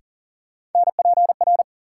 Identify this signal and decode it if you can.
Morse code, heard as a keyed tone, about 40 wpm, 700 Hz. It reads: NPR
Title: Sets of 5 Words - New Word Drill - 40wpm